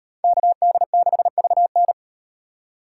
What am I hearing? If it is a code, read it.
Morse code, heard as a keyed tone, about 36 wpm, 700 Hz. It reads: KD6VN